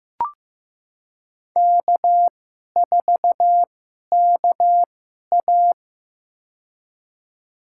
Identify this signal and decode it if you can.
Morse code, heard as a keyed tone, about 15 wpm, spaced as for 11 wpm, 700 Hz. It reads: K4KA
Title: USA Call Signs - 4 Characters - 2x Character Spacing - 15wpm